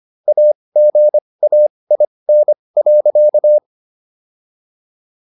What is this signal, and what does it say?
Morse code, heard as a keyed tone, about 25 wpm, spaced as for 20 wpm, 600 Hz. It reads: AGAIN.